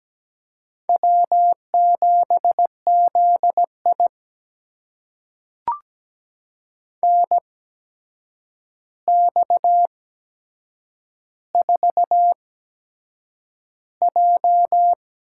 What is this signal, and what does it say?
Morse code, heard as a keyed tone, about 17 wpm, spaced as for 7 wpm, 700 Hz. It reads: W7ZI NX4J
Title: USA Call Signs - 4 Characters - 8x Character Spacing - 17wpm